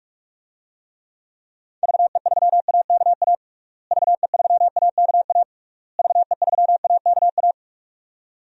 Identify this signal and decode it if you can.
Morse code, heard as a keyed tone, about 46 wpm, 700 Hz. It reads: VE3AKA VE3AKA VE3AKA
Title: DXCC-100 Call Signs - 6 Characters - Repeated 3x - 45wpm